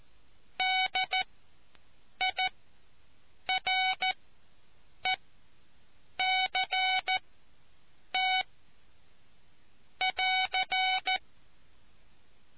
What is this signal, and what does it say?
Morse code, heard as a keyed tone, about 14 wpm, spaced as for 7 wpm, 750 Hz. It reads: DIRECT <AR>